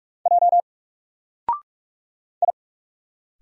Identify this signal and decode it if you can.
Morse code, heard as a keyed tone, about 46 wpm, 700 Hz. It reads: J I
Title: Single Letter-Number 45wpm